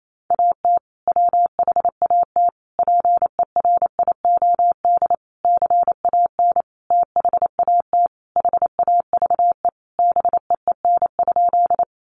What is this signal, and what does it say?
Morse code, heard as a keyed tone, about 28 wpm, 700 Hz. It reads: AT WHAT PERIOD CAN THAT HAVE BEEN?
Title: Sentences from Top 800 Words 28wpm